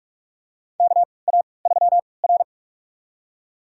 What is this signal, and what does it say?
Morse code, heard as a keyed tone, about 46 wpm, spaced as for 26 wpm, 700 Hz. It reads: KA3R
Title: USA Call Signs - 4 Characters - 3x Character Spacing - 45wpm